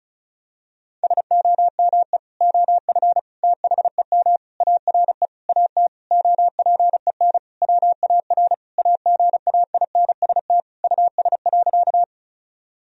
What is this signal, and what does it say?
Morse code, heard as a keyed tone, about 34 wpm, 700 Hz. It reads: SOME OF THEM ARE AT OPEN WAR AGAINST US.
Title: Sentences from Top 1000 Words - Review Part 2 - 35wpm